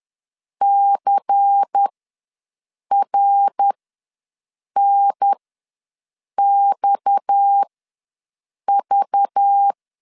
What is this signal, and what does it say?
Morse code, heard as a keyed tone, about 10 wpm, spaced as for 6 wpm, 800 Hz. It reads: CRNXV
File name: LEZIONI / 6 PKX0 / 6 PKX ws 10wpm 3706